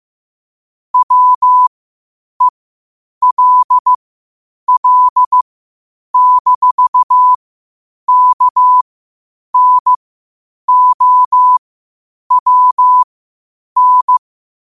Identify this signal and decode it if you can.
Morse code, heard as a keyed tone, about 15 wpm, spaced as for 8 wpm, 1000 Hz. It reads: WELL-KNOWN
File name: qotdfiles / 8wpm